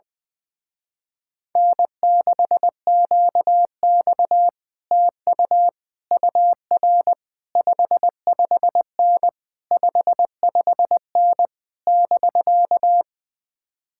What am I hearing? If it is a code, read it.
Morse code, heard as a keyed tone, about 20 wpm, 700 Hz. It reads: N6QX TU UR 55N 55N <BK>
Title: Speed-Racing Common QSO Phrases - 20wpm